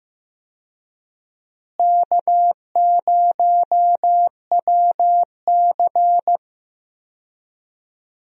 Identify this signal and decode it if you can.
Morse code, heard as a keyed tone, about 15 wpm, 700 Hz. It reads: K0WC
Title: USA Call Signs - 4 Characters - Repeated 3x - 15wpm